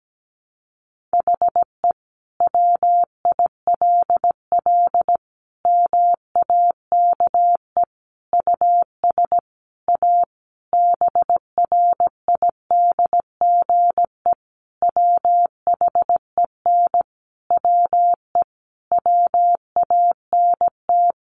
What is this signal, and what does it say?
Morse code, heard as a keyed tone, about 17 wpm, 700 Hz. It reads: HE WILL MAKE US A BRIDGE WHEN WE WANT
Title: Sentences from Top 1200 Words 17wpm